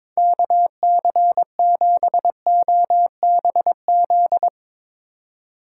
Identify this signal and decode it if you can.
Morse code, heard as a keyed tone, about 22 wpm, 700 Hz. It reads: KC7OBZ
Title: Speed Racing - USA 2x3 Call Signs 22wpm